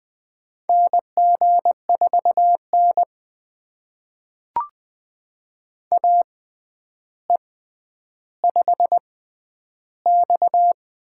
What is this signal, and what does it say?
Morse code, heard as a keyed tone, about 20 wpm, spaced as for 10 wpm, 700 Hz. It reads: NG4N AE5X
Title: USA Call Signs - 4 Characters - 6x Character Spacing - 20wpm